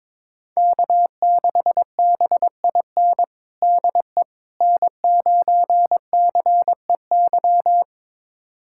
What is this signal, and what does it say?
Morse code, heard as a keyed tone, about 22 wpm, 700 Hz. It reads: K6BIN DE N9CEY
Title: USA Callsign DE Callsign - 5 Characters - 22wpm